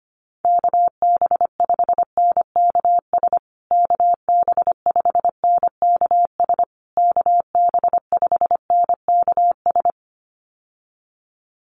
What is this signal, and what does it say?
Morse code, heard as a keyed tone, about 25 wpm, 700 Hz. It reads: KB5NKS KB5NKS KB5NKS
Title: USA Call Signs Repeated 3x 25wpm